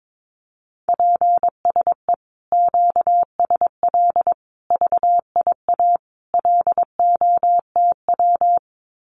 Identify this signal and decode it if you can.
Morse code, heard as a keyed tone, about 22 wpm, 700 Hz. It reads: PSE QSL VIA LOTW
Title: Common QSO Phrases Encore 22wpm